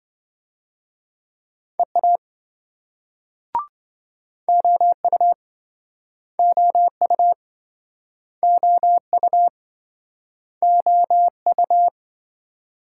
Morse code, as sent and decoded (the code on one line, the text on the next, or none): . .- / --- ..- / --- ..- / --- ..- / --- ..-
EA OU OU OU OU